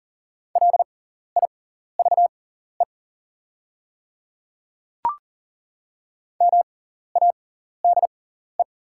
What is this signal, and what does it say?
Morse code, heard as a keyed tone, about 40 wpm, spaced as for 14 wpm, 700 Hz. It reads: LIVE MADE
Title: 6x Character Spacing - Top 300 Words - 40wpm